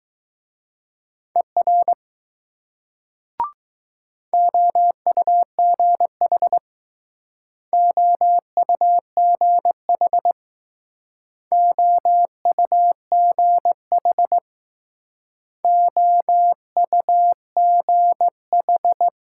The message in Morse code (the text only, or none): ER OUGH OUGH OUGH OUGH